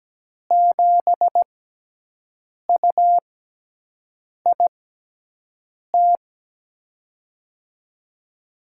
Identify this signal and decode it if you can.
Morse code, heard as a keyed tone, about 17 wpm, spaced as for 6 wpm, 700 Hz. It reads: 7UIT
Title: USA Call Signs - 6 Characters - 6x Character Spacing - 17wpm